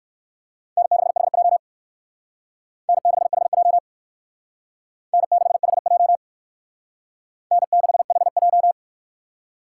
Morse code, heard as a keyed tone, about 56 wpm, 700 Hz. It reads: N6HJ N6HJ N6HJ N6HJ